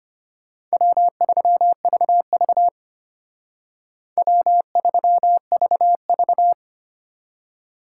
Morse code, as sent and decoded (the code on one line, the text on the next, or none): .-- ...-- ...- ...- / .-- ...-- ...- ...-
W3VV W3VV